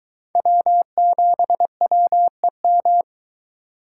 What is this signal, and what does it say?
Morse code, heard as a keyed tone, about 22 wpm, 700 Hz. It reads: W7WEM